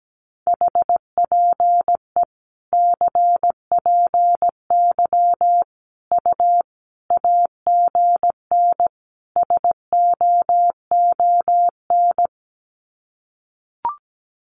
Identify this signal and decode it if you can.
Morse code, heard as a keyed tone, about 17 wpm, 700 Hz. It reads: HPE CPY U AGN SOON
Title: Common QSO Phrases 17wpm